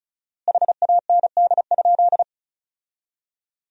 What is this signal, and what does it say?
Morse code, heard as a keyed tone, about 34 wpm, 700 Hz. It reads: HAND?
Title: 4x Word-Spacing - Sentences from Top 200 Words - 35wpm